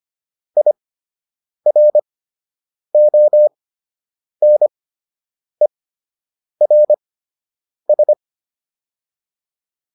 Morse code, heard as a keyed tone, about 25 wpm, spaced as for 8 wpm, 600 Hz. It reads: IRONERS